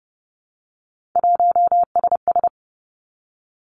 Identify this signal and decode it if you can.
Morse code, heard as a keyed tone, about 31 wpm, 700 Hz. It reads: 1SS
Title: Three Challenging Letter-Number Combinations 30wpm